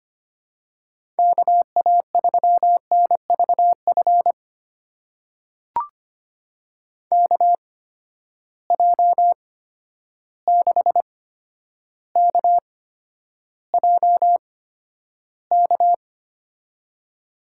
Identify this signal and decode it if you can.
Morse code, heard as a keyed tone, about 25 wpm, spaced as for 11 wpm, 700 Hz. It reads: KA3NVF KJ6KJK
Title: USA Call Signs - 6 Characters - 8x Character Spacing - 25wpm